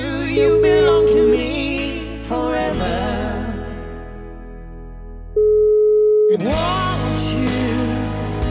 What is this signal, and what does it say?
Ringback tone (Europe, 425 Hz), with noise.